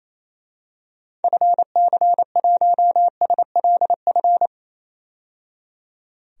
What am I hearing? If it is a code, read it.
Morse code, heard as a keyed tone, about 28 wpm, 700 Hz. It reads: FC1SLF
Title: DXCC-100 Call Signs - 6 Characters - 28wpm